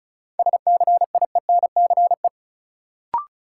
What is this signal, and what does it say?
Morse code, heard as a keyed tone, about 36 wpm, 700 Hz. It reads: SCIENCE
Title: Sentences from Top 1000 Words - Review Part 2 - New Word Drill - 35wpm